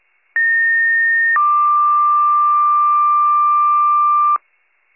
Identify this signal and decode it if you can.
Two-tone page, heard as a one-second tone, then a three-second tone, about 1850 then 1180 Hz. It is heard from 0.4 to 4.4 s.